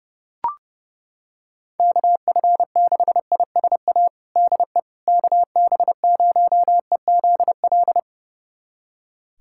Morse code, heard as a keyed tone, about 30 wpm, 700 Hz. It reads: KF6ISA DE KB0EZL
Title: USA Callsign DE Callsign - 6 Characters - 30wpm